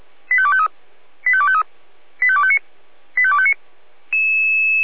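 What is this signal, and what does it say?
Five-tone selective call (ZVEI), heard as quick runs of five tones, tones of about 70 ms. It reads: 85253 85253 85259 85259
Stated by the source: ZVEI five-tone selective call received by SDR.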